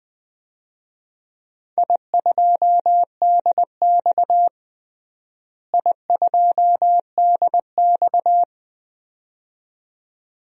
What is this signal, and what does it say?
Morse code, heard as a keyed tone, about 20 wpm, 700 Hz. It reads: I2DX I2DX
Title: DXCC-100 Call Signs - 4 Characters - Repeated 2x - 20wpm